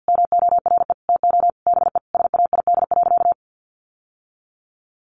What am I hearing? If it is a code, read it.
Morse code, heard as a keyed tone, about 50 wpm, 700 Hz. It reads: MORE TO BE SAID.
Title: Sentences from Top 1000 Words 50wpm